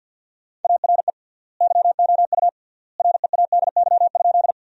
Morse code, heard as a keyed tone, about 50 wpm, 700 Hz. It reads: ARE YOU READY?
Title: 3x Word-Spacing - Sentences from Top 400 Words - 50wpm